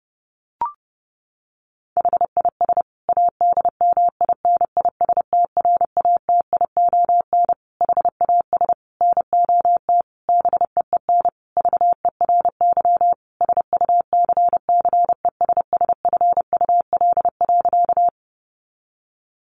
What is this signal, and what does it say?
Morse code, heard as a keyed tone, about 28 wpm, 700 Hz. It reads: HIS ADMINISTRATION HAS NOT BEEN VERY SUCCESSFUL.
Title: Sentences from Top 1300 Words 30wpm